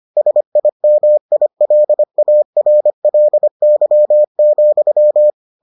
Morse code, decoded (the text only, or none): SIMILARLY,